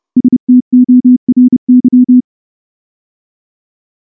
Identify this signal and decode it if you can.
Morse code, heard as a keyed tone, about 28 wpm, 250 Hz. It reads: STORY
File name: morse-training-30WPM